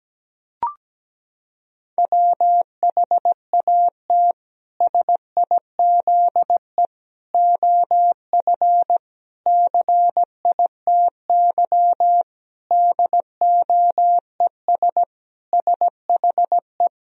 Morse code, heard as a keyed tone, about 17 wpm, 700 Hz. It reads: WHAT SIZE OF CITY DOES SHE
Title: Sentences from Top 1000 Words - Review Part 2 - 17wpm